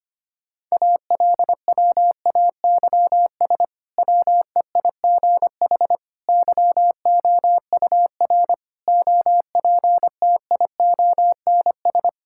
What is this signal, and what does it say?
Morse code, heard as a keyed tone, about 25 wpm, 700 Hz. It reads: ALWAYS WEIGH YOUR OPTIONS
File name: practice / 25wpm / Difficult Sentences with Hints - 25wpm